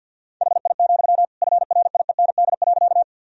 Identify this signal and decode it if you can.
Morse code, heard as a keyed tone, about 47 wpm, 700 Hz. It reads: HI, FRIEND.
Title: Sentences from Top 1000 Words - Review Part 3 - 50wpm